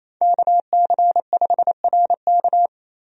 Morse code, heard as a keyed tone, about 28 wpm, 700 Hz. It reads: KC5RK
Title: USA Call Signs - 5 Characters - 28wpm